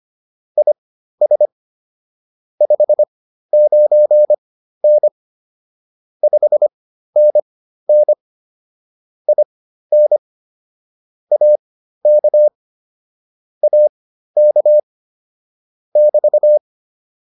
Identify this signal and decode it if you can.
Morse code, heard as a keyed tone, about 25 wpm, spaced as for 13 wpm, 600 Hz. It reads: IS 59N 5NN IN AK AK <BT>